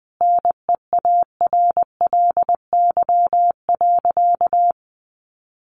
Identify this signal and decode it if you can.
Morse code, heard as a keyed tone, about 19 wpm, 700 Hz. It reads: NEARLY.
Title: Sentences from Top 1300 Words 20wpm